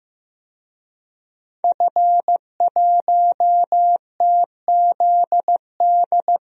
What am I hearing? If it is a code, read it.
Morse code, heard as a keyed tone, about 15 wpm, 700 Hz. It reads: F1TZD
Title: DXCC-100 Call Signs - 5 Characters - 15wpm